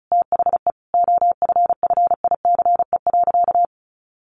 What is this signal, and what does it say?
Morse code, heard as a keyed tone, about 34 wpm, 700 Hz. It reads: THE OFFICE.
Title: Sentences from Top 1000 Words 35wpm